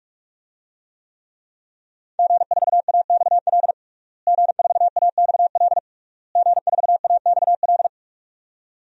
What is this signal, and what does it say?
Morse code, heard as a keyed tone, about 46 wpm, 700 Hz. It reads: G4AXL G4AXL G4AXL
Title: DXCC-100 Call Signs - 5 Characters - Repeated 3x - 45wpm